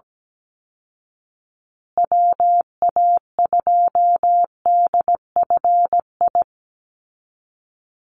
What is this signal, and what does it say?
Morse code, heard as a keyed tone, about 17 wpm, 700 Hz. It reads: WA2DFI
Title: Call Signs Repeated 2x - 17wpm